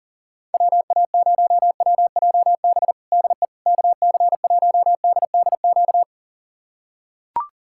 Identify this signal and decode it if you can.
Morse code, heard as a keyed tone, about 40 wpm, 700 Hz. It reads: WA0WJB DE KC1DDQ
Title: USA Callsign DE Callsign - 6 Characters - 40wpm